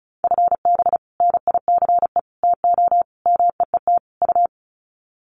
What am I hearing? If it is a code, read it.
Morse code, heard as a keyed tone, about 34 wpm, 700 Hz. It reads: FB NICE TO MEET U
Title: Common QSO Phrases 35wpm